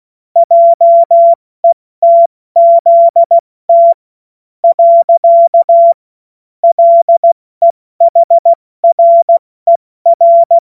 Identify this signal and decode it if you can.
Morse code, heard as a keyed tone, about 16 wpm, 680 Hz. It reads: JETZT . LEHRER
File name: Morsekurs / Witze / Witz 10. August 2021, 70 F80